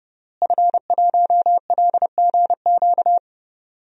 Morse code, heard as a keyed tone, about 30 wpm, 700 Hz. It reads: F1LGQ
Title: DXCC-100 Call Signs - 5 Characters - Repeated 2x - 30wpm